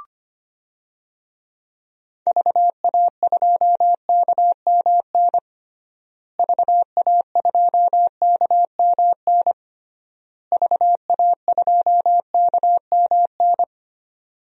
Morse code, heard as a keyed tone, about 25 wpm, 700 Hz. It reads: VA2KMN VA2KMN VA2KMN
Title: DXCC-100 Call Signs - 6 Characters - Repeated 3x - 25wpm